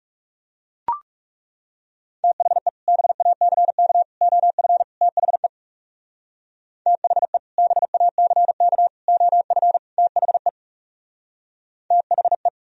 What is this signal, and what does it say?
Morse code, heard as a keyed tone, about 40 wpm, 700 Hz. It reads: THE BACK OF THE THE BACK OF THE THE